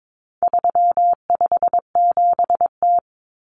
Morse code, heard as a keyed tone, about 22 wpm, 700 Hz. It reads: 357T
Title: Common Cut Numbers - 4-Digit 22wpm